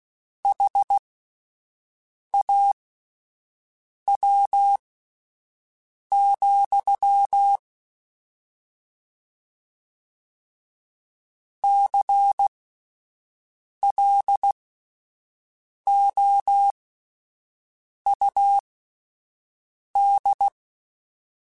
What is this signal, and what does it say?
Morse code, heard as a keyed tone, about 16 wpm, spaced as for 5 wpm, 800 Hz. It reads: HAW, CLOUD
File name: Misc / Misc-mp3s / Word Pairs @5